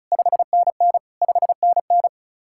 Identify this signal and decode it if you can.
Morse code, heard as a keyed tone, about 42 wpm, 700 Hz. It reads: 5NN 5NN
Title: Speed-Racing Common QSO Phrases - 35wpm